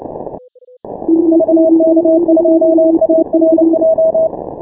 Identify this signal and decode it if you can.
Morse code, heard as a keyed tone, about 29 wpm, 640 Hz. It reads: VK2ARJ
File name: final / W1UJ-168-140-MorseRunner